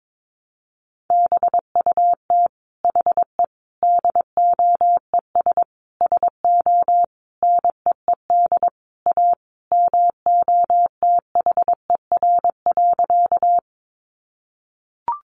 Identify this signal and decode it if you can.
Morse code, heard as a keyed tone, about 22 wpm, 700 Hz. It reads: BUT HE DOES SO NEED A MOTHER.
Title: Sentences from Top 200 Words 22wpm